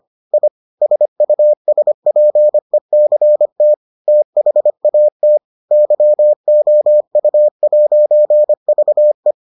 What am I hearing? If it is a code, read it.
Morse code, heard as a keyed tone, about 25 wpm, 600 Hz. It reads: I SUSPECT THAT YOU'VE